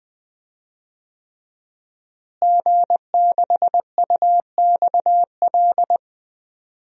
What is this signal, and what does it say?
Morse code, heard as a keyed tone, about 20 wpm, 700 Hz. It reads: G6UXL
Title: DXCC-100 Call Signs - 5 Characters - Repeated 3x - 20wpm